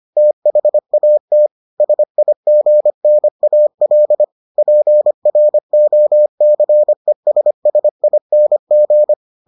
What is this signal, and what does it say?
Morse code, heard as a keyed tone, about 25 wpm, 600 Hz. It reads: THAT SIGNAL PROCESSING